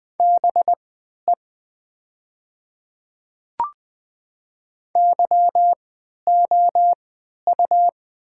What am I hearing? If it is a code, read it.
Morse code, heard as a keyed tone, about 20 wpm, spaced as for 11 wpm, 700 Hz. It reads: BE YOU
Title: 3x Character Spacing - Top 200 Words - 20wpm